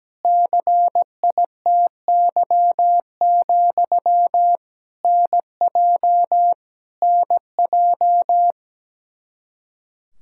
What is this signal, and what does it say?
Morse code, heard as a keyed tone, about 17 wpm, 700 Hz. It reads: CITY, NJ NJ
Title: Practice QSO - Exchange 1 - Beginning - 17wpm